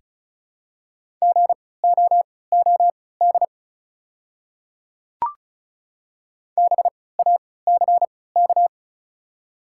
Morse code, heard as a keyed tone, about 36 wpm, spaced as for 20 wpm, 700 Hz. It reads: GOOD BACK
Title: 3x Character Spacing - Top 200 Words - 35wpm